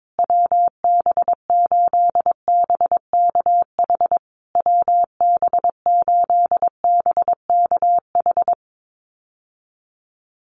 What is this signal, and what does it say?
Morse code, heard as a keyed tone, about 22 wpm, 700 Hz. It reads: WB8BKH WB8BKH
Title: USA Call Signs Repeated 3x 22wpm